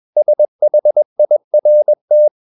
Morse code, heard as a keyed tone, about 21 wpm, 600 Hz. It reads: SHIRT
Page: CW med Gnister